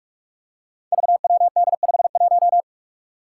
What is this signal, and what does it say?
Morse code, heard as a keyed tone, about 46 wpm, 700 Hz. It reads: VWD51
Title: Five Challenging Letter-Number Combinations 45wpm